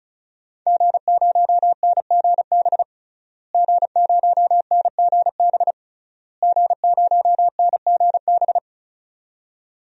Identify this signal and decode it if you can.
Morse code, heard as a keyed tone, about 34 wpm, 700 Hz. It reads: G0NGB G0NGB G0NGB
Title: DXCC-100 Call Signs - 5 Characters - Repeated 3x - 35wpm